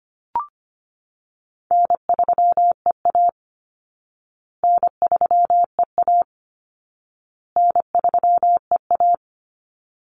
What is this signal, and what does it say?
Morse code, heard as a keyed tone, about 25 wpm, 700 Hz. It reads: N3EA N3EA N3EA
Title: Easy Call Signs Repeated 3x - 25wpm